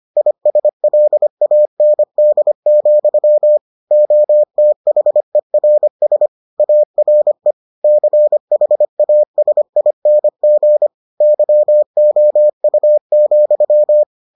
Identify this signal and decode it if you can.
Morse code, heard as a keyed tone, about 25 wpm, 600 Hz. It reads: ISLAND, OTHERS ARE CHASING YOU,